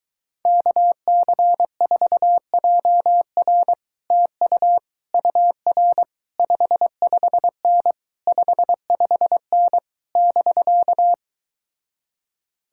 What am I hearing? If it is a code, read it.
Morse code, heard as a keyed tone, about 23 wpm, 700 Hz. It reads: KC4JR TU UR 55N 55N <BK>